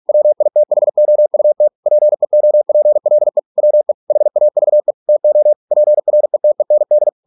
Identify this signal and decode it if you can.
Morse code, heard as a keyed tone, about 46 wpm, 600 Hz. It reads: WITHOUT PEOPLE WE HAVE TO PRETEND